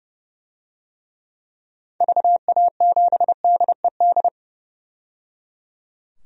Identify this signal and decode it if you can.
Morse code, heard as a keyed tone, about 30 wpm, 700 Hz. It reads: VA7DED
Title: DXCC-100 Call Signs - 6 Characters - 30wpm